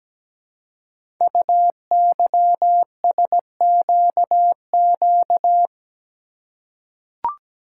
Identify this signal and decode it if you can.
Morse code, heard as a keyed tone, about 17 wpm, 700 Hz. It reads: UYSQQ
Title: Code Groups - 5 Letters - 17wpm